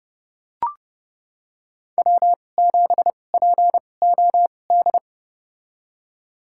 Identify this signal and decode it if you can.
Morse code, heard as a keyed tone, about 30 wpm, spaced as for 22 wpm, 700 Hz. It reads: W7POD
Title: USA Call Signs - 5 Characters - 2x Character Spacing - 30wpm